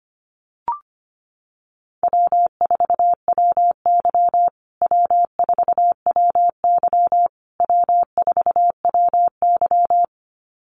Morse code, heard as a keyed tone, about 25 wpm, 700 Hz. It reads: W4WY W4WY W4WY
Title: USA Call Signs Repeated 3x 25wpm